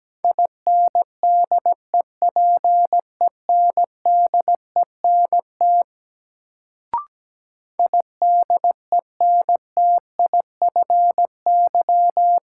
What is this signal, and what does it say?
Morse code, heard as a keyed tone, about 17 wpm, 700 Hz. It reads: INDEPENDENT IDENTIFY
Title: Sentences from Top 900 Words - New Word Drill - 17wpm